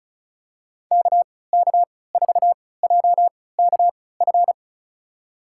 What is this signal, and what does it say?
Morse code, heard as a keyed tone, about 36 wpm, spaced as for 20 wpm, 700 Hz. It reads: KK4JKF
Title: USA Call Signs - 6 Characters - 3x Character Spacing - 35wpm